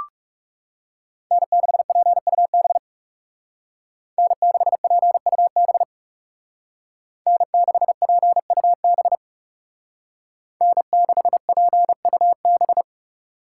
Morse code, heard as a keyed tone, about 36 wpm, 700 Hz. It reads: N6PUB N6PUB N6PUB N6PUB
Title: Speed Racing - USA 1x3 Call Signs 30wpm